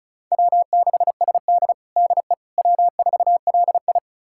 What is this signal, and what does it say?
Morse code, heard as a keyed tone, about 36 wpm, 700 Hz. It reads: W6SD DE W4LI
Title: USA Callsign DE Callsign - 4 Characters - 35wpm